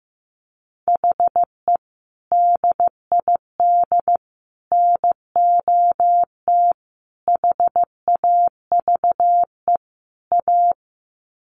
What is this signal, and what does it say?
Morse code, heard as a keyed tone, about 15 wpm, 700 Hz. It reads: HE DID NOT HAVE A
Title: Sets of 5 Words Encore 15wpm